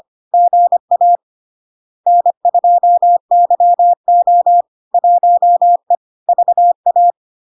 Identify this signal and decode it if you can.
Morse code, heard as a keyed tone, about 25 wpm, 700 Hz. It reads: GA N2YO 1E VA
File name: LARK-FD-2022-Log-in-CW-25wpm-700-Hz